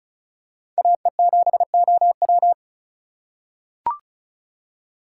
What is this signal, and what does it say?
Morse code, heard as a keyed tone, about 36 wpm, 700 Hz. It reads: AE7OW
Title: USA Call Signs - 5 Characters - 2x Character Spacing - 35wpm